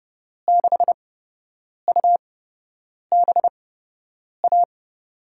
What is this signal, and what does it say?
Morse code, heard as a keyed tone, about 30 wpm, spaced as for 8 wpm, 700 Hz. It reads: 6UBA
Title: USA Call Signs - 6 Characters - 8x Character Spacing - 30wpm